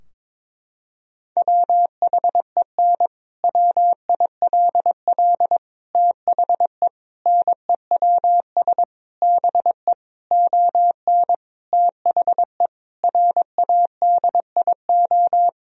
Morse code, heard as a keyed tone, about 22 wpm, 700 Hz. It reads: WHEN WILL THE NEWS BE ON THE RADIO
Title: Sentences from Top 1000 Words - Review Part 3 - 22wpm